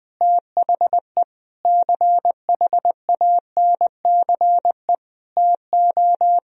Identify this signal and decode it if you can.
Morse code, heard as a keyed tone, about 20 wpm, 700 Hz. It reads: THE CHANCE TO
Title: Sentences from Top 1000 Words - Review Part 4 - 20wpm